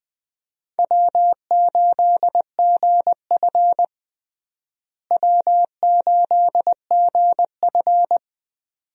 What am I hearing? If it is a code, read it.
Morse code, heard as a keyed tone, about 20 wpm, 700 Hz. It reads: W8GF W8GF
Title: USA Call Signs - 4 Characters - Repeated 2x - 20wpm